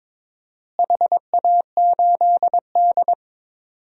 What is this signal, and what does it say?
Morse code, heard as a keyed tone, about 22 wpm, 700 Hz. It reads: HA8D
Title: DXCC-100 Call Signs - 4 Characters - Repeated 3x - 22wpm